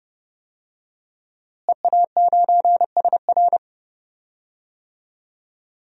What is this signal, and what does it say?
Morse code, heard as a keyed tone, about 30 wpm, 700 Hz. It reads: EA9SR
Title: DXCC-100 Call Signs - 5 Characters - Repeated 2x - 30wpm